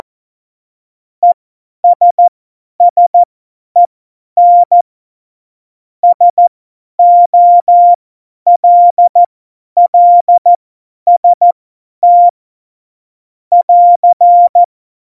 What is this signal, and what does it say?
Morse code, heard as a keyed tone, about 14 wpm, spaced as for 10 wpm, 700 Hz. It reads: ESSEN SOLLST <AR>
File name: Morsekurs / Witze / Witz 18. September 2018, 50 F70